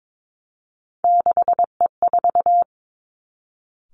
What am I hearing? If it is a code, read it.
Morse code, heard as a keyed tone, about 22 wpm, 700 Hz. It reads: 6E4
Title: Three Challenging Letter-Number Combinations 22wpm